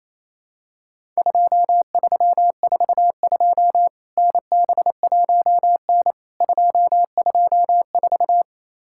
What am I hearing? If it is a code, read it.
Morse code, heard as a keyed tone, about 28 wpm, 700 Hz. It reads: 2342 NB1N 224
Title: Worked All Europe DX Contest - QTC Exchange - 28wpm